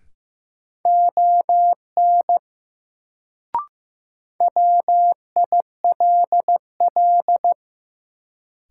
Morse code, heard as a keyed tone, about 15 wpm, 700 Hz. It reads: ON WILL
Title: Sentences from Top 100 Words - New Word Drill - 15wpm